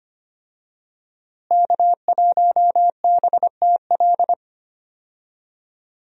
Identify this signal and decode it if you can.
Morse code, heard as a keyed tone, about 25 wpm, 700 Hz. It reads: K1BTL
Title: USA Call Signs - 5 Characters - Repeated 2x - 25wpm